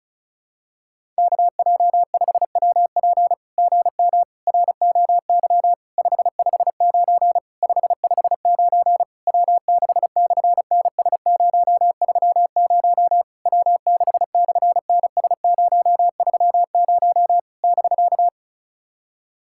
Morse code, heard as a keyed tone, about 33 wpm, 700 Hz. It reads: KJ5WP GM ROY 559 559 W6/NS030 W6/NS030 <BK>
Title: SOTA QSO Practice 35wpm